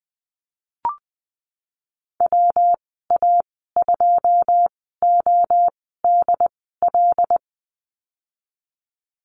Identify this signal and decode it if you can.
Morse code, heard as a keyed tone, about 20 wpm, spaced as for 15 wpm, 700 Hz. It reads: WA2ODL
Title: 154 Lesson 25g - Letter p - Call Signs 20wpm